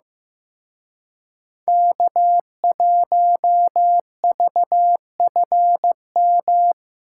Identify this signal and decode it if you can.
Morse code, heard as a keyed tone, about 18 wpm, 700 Hz. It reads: K1VFM
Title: Speed Racing - USA 1x3 Call Signs 15wpm